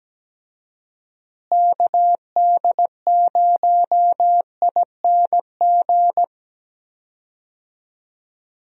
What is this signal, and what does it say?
Morse code, heard as a keyed tone, about 17 wpm, 700 Hz. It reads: KD0ING
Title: USA Call Signs - 6 Characters - Repeated 2x - 17wpm